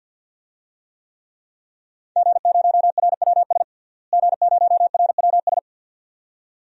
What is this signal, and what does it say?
Morse code, heard as a keyed tone, about 52 wpm, 700 Hz. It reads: G0RWS G0RWS
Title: DXCC-100 Call Signs - 5 Characters - Repeated 2x - 50wpm